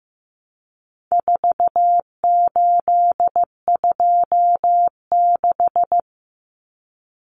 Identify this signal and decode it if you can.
Morse code, heard as a keyed tone, about 15 wpm, 700 Hz. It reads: 4826
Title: Numbers - 4-Digit 15wpm